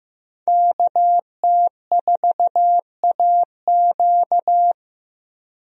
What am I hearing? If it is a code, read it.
Morse code, heard as a keyed tone, about 15 wpm, 700 Hz. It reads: KT4AQ
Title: USA Call Signs - 5 Characters - Repeated 3x - 15wpm